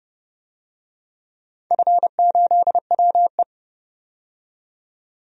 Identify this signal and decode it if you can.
Morse code, heard as a keyed tone, about 32 wpm, 700 Hz. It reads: F8WE
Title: DXCC-100 Call Signs - 4 Characters - 30wpm